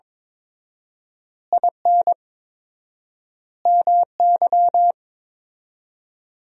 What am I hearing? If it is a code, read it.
Morse code, heard as a keyed tone, about 22 wpm, 700 Hz. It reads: IN MY
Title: 4x Word-Spacing - Sentences from Top 500 Words - 22wpm